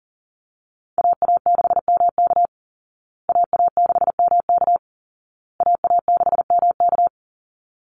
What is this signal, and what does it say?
Morse code, heard as a keyed tone, about 40 wpm, 700 Hz. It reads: AA6MK AA6MK AA6MK
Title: Call Signs Repeated 2x - 40wpm